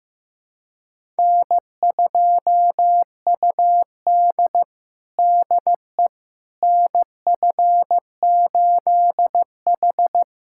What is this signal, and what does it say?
Morse code, heard as a keyed tone, about 15 wpm, 700 Hz. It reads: N2UD DE NF8H
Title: USA Callsign DE Callsign - 4 Characters - 15wpm